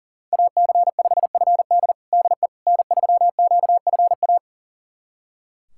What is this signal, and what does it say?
Morse code, heard as a keyed tone, about 40 wpm, 700 Hz. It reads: AC5FD DE N3QFA
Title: USA Callsign DE Callsign - 5 Characters - 40wpm